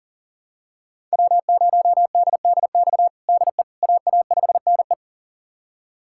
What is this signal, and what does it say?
Morse code, heard as a keyed tone, about 40 wpm, 700 Hz. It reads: W0DDX DE AA5NE
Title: USA Callsign DE Callsign - 5 Characters - 40wpm